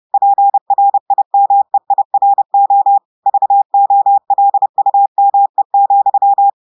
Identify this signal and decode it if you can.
Morse code, heard as a keyed tone, about 30 wpm, 800 Hz. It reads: PRIMEIRO VOLUME,
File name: audios / portugais / portugais texte en clair 1 vitesse 30 mots minutes